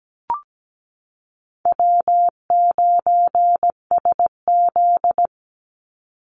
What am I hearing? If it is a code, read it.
Morse code, heard as a keyed tone, about 17 wpm, 700 Hz. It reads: W9SZ
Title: Call Signs Repeated 3x - 17wpm